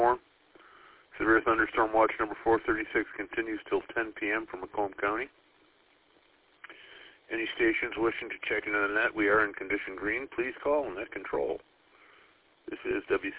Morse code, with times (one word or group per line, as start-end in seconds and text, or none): none